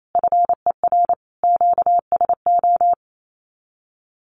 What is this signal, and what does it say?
Morse code, heard as a keyed tone, about 28 wpm, 700 Hz. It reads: FER QSO
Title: Common QSO Phrases 28wpm